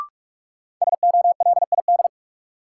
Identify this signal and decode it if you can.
Morse code, heard as a keyed tone, about 47 wpm, 700 Hz. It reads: SOLID
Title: Sentences from Top 1500 Words - New Word Drill - 45wpm